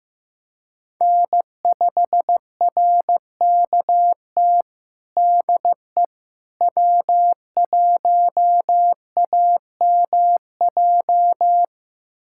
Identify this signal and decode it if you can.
Morse code, heard as a keyed tone, about 15 wpm, 700 Hz. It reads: N5RKT DE W1AMJ
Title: USA Callsign DE Callsign - 5 Characters - 15wpm